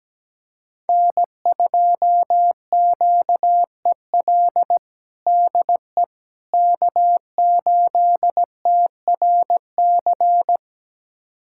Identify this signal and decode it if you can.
Morse code, heard as a keyed tone, about 17 wpm, 700 Hz. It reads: N2QEL DE K8TRC
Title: USA Callsign DE Callsign - 5 Characters - 17wpm